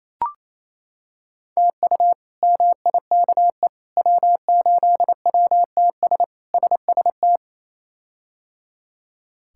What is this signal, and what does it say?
Morse code, heard as a keyed tone, about 27 wpm, 700 Hz. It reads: TU MIKE W8WTS SST
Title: K1USN SST QSO Practice - 28wpm